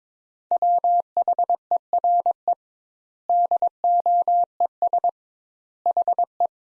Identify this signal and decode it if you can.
Morse code, heard as a keyed tone, about 22 wpm, 700 Hz. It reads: WHERE DOES HE
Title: Sentences Top 200 Words - 22wpm